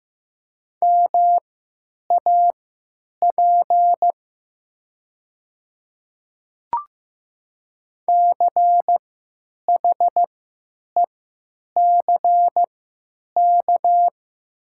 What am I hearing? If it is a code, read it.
Morse code, heard as a keyed tone, about 15 wpm, spaced as for 9 wpm, 700 Hz. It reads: MAP CHECK
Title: 3x Character Spacing - Top 500 Words - 15wpm